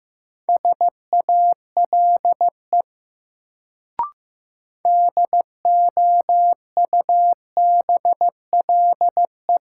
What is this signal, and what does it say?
Morse code, heard as a keyed tone, about 15 wpm, 700 Hz. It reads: SALE DOUBLE
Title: Sentences from Top 1400 Words - New Word Drill - 15wpm